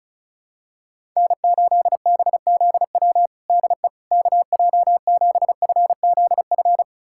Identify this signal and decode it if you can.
Morse code, heard as a keyed tone, about 36 wpm, 700 Hz. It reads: N8BZW DE KJ7FZF
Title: USA Callsign DE Callsign 35wpm